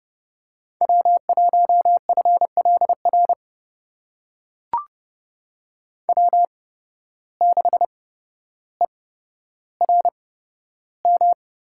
Morse code, heard as a keyed tone, about 30 wpm, spaced as for 13 wpm, 700 Hz. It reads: W1FLR W6ERM
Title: USA Call Signs - 5 Characters - 8x Character Spacing - 30wpm